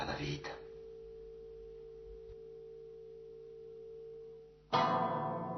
Dial tone (China, 450 Hz), with noise.